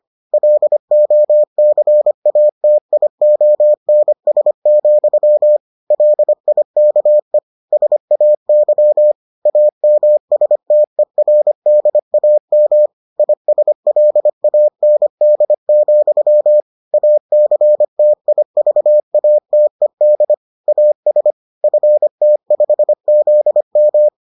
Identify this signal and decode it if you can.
Morse code, heard as a keyed tone, about 25 wpm, 600 Hz. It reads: LOCATIONS, LIKE SAY AMSTERDAM ISLAND, ACTIVATED AS FT5ZM